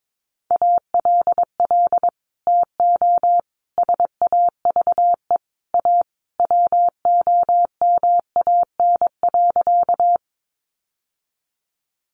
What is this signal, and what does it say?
Morse code, heard as a keyed tone, about 22 wpm, 700 Hz. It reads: ALL TO SAVE A WOMAN.
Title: Sentences from Top 1100 Words 22wpm